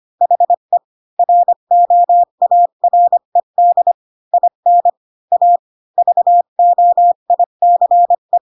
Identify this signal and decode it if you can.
Morse code, heard as a keyed tone, about 25 wpm, 700 Hz. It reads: HE ROARED IN A VOICE
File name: Pinocchio / Pinocchio ch3 20 25 ik1hge